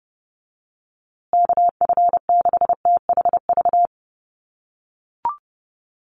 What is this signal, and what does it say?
Morse code, heard as a keyed tone, about 30 wpm, 700 Hz. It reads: KF6THV
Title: USA Call Signs Repeated 2x 30wpm